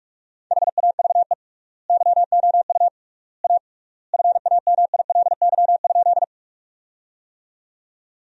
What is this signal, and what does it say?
Morse code, heard as a keyed tone, about 41 wpm, 700 Hz. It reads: HAVE YOU A FAMILY?
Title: 3x Word-Spacing - Sentences from Top 400 Words - 45wpm